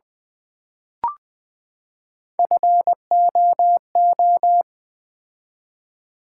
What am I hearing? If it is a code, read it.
Morse code, heard as a keyed tone, about 20 wpm, 700 Hz. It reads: FOO
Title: Build A Word - 3 Letters At A Time - 8-Letter Words - 20wpm